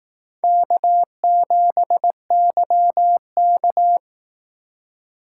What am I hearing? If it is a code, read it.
Morse code, heard as a keyed tone, about 19 wpm, 700 Hz. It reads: K7YK
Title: Speed Racing - USA 1x2 Call Signs 15wpm